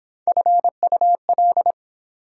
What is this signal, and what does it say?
Morse code, heard as a keyed tone, about 24 wpm, 700 Hz. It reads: FUL